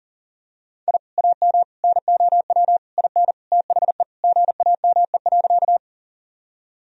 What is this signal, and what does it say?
Morse code, heard as a keyed tone, about 40 wpm, 700 Hz. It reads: I AM NOW IN THE GAME.
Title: Sentences from Top 1000 Words - Review Part 3 - 40wpm